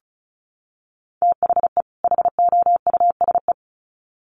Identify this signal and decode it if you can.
Morse code, heard as a keyed tone, about 34 wpm, 700 Hz. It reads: THE HOUSE
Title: Sets of 2 Words 35wpm